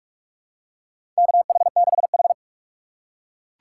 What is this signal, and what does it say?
Morse code, heard as a keyed tone, about 44 wpm, 700 Hz. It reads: KH6H
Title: USA Call Signs - 4 Characters - 3x Character Spacing - 45wpm